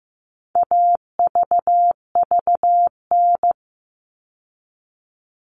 Morse code, heard as a keyed tone, about 15 wpm, 700 Hz. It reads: AVVN
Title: All Cut Numbers - 4-Digit 15wpm